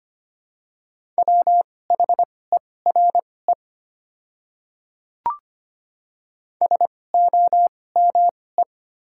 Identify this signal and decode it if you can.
Morse code, heard as a keyed tone, about 25 wpm, spaced as for 18 wpm, 700 Hz. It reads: WHERE SOME
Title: 2x Character Spacing - Top 200 Words - 25wpm